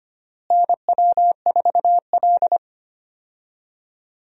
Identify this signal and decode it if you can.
Morse code, heard as a keyed tone, about 25 wpm, 700 Hz. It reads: NW4L
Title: USA Call Signs - 4 Characters - Repeated 3x - 25wpm